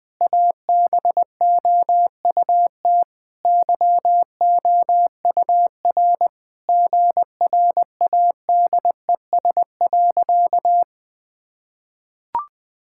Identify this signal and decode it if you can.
Morse code, heard as a keyed tone, about 20 wpm, 700 Hz. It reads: ABOUT YOUR GRADES.
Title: Sentences with 3-Word Phrases - 20wpm